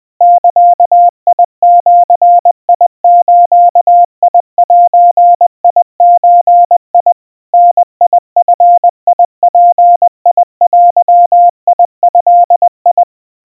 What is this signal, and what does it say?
Morse code (Japanese Wabun code), heard as a keyed tone, about 20 wpm, 700 Hz. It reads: ザジズゼゾ ダヂヅデド
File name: kawa / Morse sample / aiueo 70 cpm 700 Hz Crispy